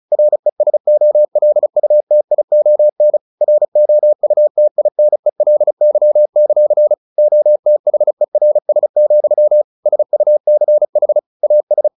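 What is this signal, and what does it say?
Morse code, heard as a keyed tone, about 36 wpm, 600 Hz. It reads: RESOLUTION ROUTINELY; OTHERS, SUCH AS
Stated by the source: Morse practice at 35 wpm from CW med Gnister, sending Ken Follett's novel Never in English.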